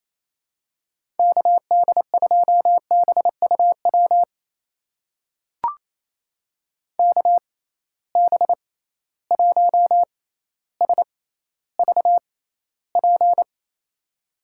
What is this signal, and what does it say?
Morse code, heard as a keyed tone, about 28 wpm, spaced as for 14 wpm, 700 Hz. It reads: KD2BUW KB1SVP
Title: USA Call Signs - 6 Characters - 6x Character Spacing - 28wpm